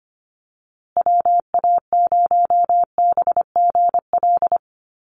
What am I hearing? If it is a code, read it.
Morse code, heard as a keyed tone, about 25 wpm, 700 Hz. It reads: WA0BGL
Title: USA Call Signs Repeated 3x 25wpm